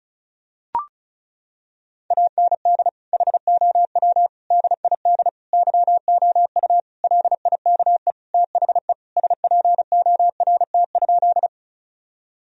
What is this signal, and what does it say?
Morse code, heard as a keyed tone, about 34 wpm, 700 Hz. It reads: AND HOW DID YOU LIKE THE SPORT?
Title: Sentences from Top 1000 Words - Review Part 4 - 35wpm